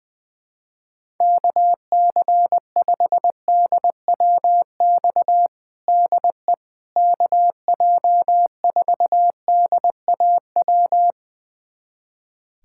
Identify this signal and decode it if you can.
Morse code, heard as a keyed tone, about 20 wpm, 700 Hz. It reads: KC5DWX DE KJ4DAW
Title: USA Callsign DE Callsign - 6 Characters - 20wpm